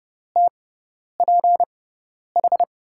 Morse code, heard as a keyed tone, about 31 wpm, spaced as for 10 wpm, 700 Hz. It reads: TPH
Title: USA Call Signs - 6 Characters - 6x Character Spacing - 30wpm